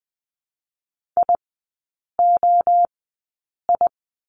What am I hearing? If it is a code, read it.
Morse code, heard as a keyed tone, about 20 wpm, 700 Hz. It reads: I O I
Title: 029 Lesson 6e - Letter i - Three Characters 20wpm